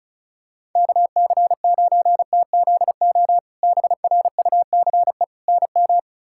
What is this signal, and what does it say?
Morse code, heard as a keyed tone, about 34 wpm, 700 Hz. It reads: KC9TZO BRUCE NM
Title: K1USN SST QSO Practice - 35wpm